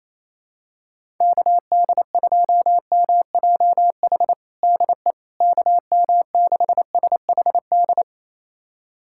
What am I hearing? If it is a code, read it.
Morse code, heard as a keyed tone, about 28 wpm, 700 Hz. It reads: KD2MJH DE KM6SHD
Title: USA Callsign DE Callsign 28wpm